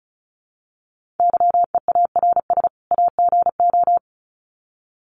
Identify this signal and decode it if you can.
Morse code, heard as a keyed tone, about 34 wpm, 700 Hz. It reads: YEARS AGO
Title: Sets of 2 Words 35wpm